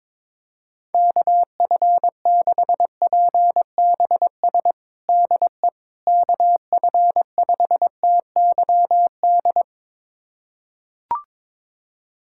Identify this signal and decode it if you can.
Morse code, heard as a keyed tone, about 22 wpm, 700 Hz. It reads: KF6PBS DE KF5TYD
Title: USA Callsign DE Callsign - 6 Characters - 22wpm